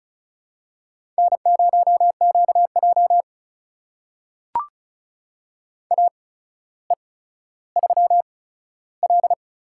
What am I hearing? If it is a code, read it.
Morse code, heard as a keyed tone, about 34 wpm, spaced as for 15 wpm, 700 Hz. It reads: N0QJ AE3L
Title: USA Call Signs - 4 Characters - 8x Character Spacing - 35wpm